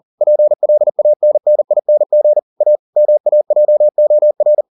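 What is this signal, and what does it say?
Morse code, heard as a keyed tone, about 40 wpm, 600 Hz. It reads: PLANNING A MAJOR
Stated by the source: Morse code at 40 wpm, practice text, CW med Gnister.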